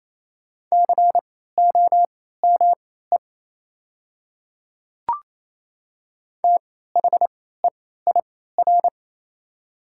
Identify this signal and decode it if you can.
Morse code, heard as a keyed tone, about 28 wpm, spaced as for 16 wpm, 700 Hz. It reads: COME THEIR
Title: 3x Character Spacing - Top 100 Words - 28wpm